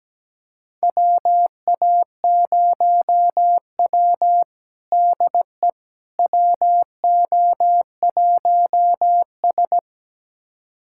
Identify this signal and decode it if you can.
Morse code, heard as a keyed tone, about 17 wpm, 700 Hz. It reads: WA0W DE WO1S
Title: USA Callsign DE Callsign - 4 Characters - 17wpm